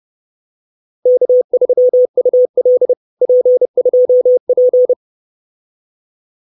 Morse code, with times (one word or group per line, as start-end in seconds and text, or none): 1.1-2.9 K3UL
3.2-4.9 P2P